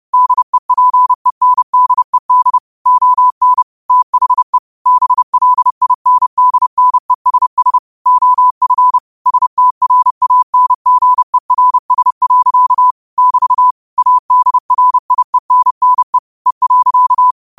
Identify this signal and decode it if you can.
Morse code, heard as a keyed tone, about 30 wpm, 1000 Hz. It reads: DEPENDED ON THE BLINDNESS OF STRANGERS. <BT> ADRIENNE E.
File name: qotdfiles / 30wpm